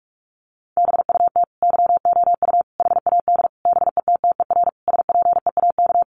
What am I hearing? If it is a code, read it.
Morse code, heard as a keyed tone, about 44 wpm, 700 Hz. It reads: BUT YOU HAD BETTER SPEAK
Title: Sentences from Top 1100 Words 45wpm